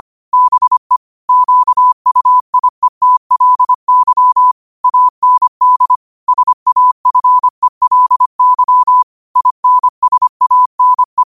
Morse code, heard as a keyed tone, about 25 wpm, 1000 Hz. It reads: BE QUIETLY AND SAFELY INSANE